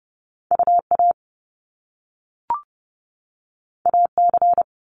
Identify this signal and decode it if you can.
Morse code, heard as a keyed tone, about 30 wpm, 700 Hz. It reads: UA AC
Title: Top Two Letter Combinations in Words 30wpm